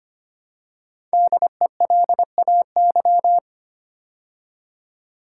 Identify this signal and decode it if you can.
Morse code, heard as a keyed tone, about 25 wpm, 700 Hz. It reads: DELAY
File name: 5 Letter Words - First 3 Letters Given - 25wpm